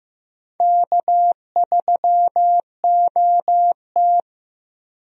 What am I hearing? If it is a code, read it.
Morse code, heard as a keyed tone, about 15 wpm, 700 Hz. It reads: K3OT
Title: USA Call Signs - 4 Characters - 15wpm